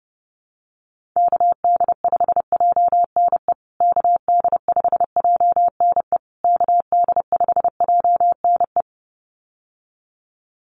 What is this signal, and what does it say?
Morse code, heard as a keyed tone, about 30 wpm, 700 Hz. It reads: KD5JNE KD5JNE KD5JNE
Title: USA Call Signs Repeated 3x 30wpm